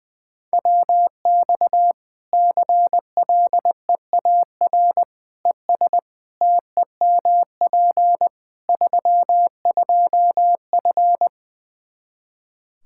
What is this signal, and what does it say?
Morse code, heard as a keyed tone, about 20 wpm, 700 Hz. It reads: WX CLEAR ES TEMP 32F
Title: Practice QSO - Exchange 2 - Info - 20wpm